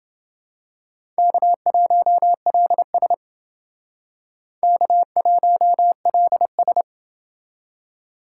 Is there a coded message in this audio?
Yes: Morse code from 1.2 to 6.8 s, heard as a keyed tone, about 28 wpm, 700 Hz. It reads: K1LS K1LS